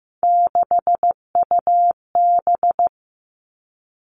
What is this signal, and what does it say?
Morse code, heard as a keyed tone, about 15 wpm, 700 Hz. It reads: 6UB
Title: Three Challenging Letter-Number Combinations 15wpm